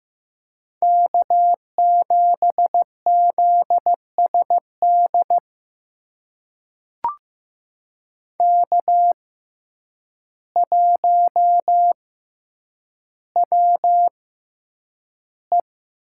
Morse code, heard as a keyed tone, about 15 wpm, spaced as for 8 wpm, 700 Hz. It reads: K7ZSD K1WE
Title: USA Call Signs - 5 Characters - 6x Character Spacing - 15wpm